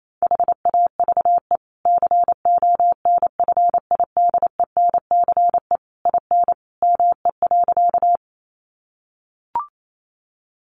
Morse code, heard as a keyed tone, about 28 wpm, 700 Hz. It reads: HAVE CONFIDENCE IN ME.
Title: Sentences from Top 1500 Words 28wpm